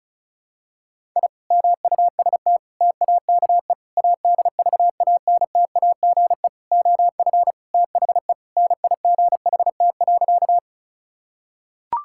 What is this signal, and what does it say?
Morse code, heard as a keyed tone, about 36 wpm, 700 Hz. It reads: I MUST TAKE ADVANTAGE OF THE NIGHT.
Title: Sentences from Top 1000 Words - Review Part 6 - 35wpm